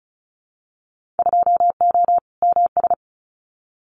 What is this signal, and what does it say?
Morse code, heard as a keyed tone, about 34 wpm, 700 Hz. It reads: 2O MS
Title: Winter Field Day Exchanges 35wpm